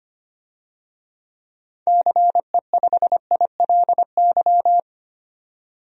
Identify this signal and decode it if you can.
Morse code, heard as a keyed tone, about 25 wpm, 700 Hz. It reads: CE5ILY
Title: DXCC-100 Call Signs - 6 Characters - 25wpm